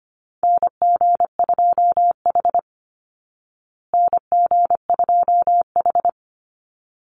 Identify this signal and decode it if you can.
Morse code, heard as a keyed tone, about 25 wpm, 700 Hz. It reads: NG2H NG2H
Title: Call Signs Repeated 3x - 25wpm